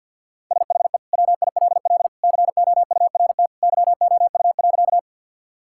Morse code, heard as a keyed tone, about 50 wpm, 700 Hz. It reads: SHE WILL COURT YOU.
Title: Sentences from Top 1600 Words 50wpm